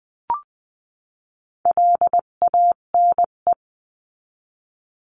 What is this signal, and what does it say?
Morse code, heard as a keyed tone, about 20 wpm, 700 Hz. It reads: LANE
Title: 258 Lesson 42c - Words 16-20wpm